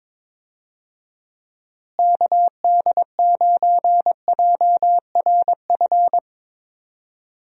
Morse code, heard as a keyed tone, about 22 wpm, 700 Hz. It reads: KD9JRF
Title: USA Call Signs - 6 Characters - Repeated 3x - 22wpm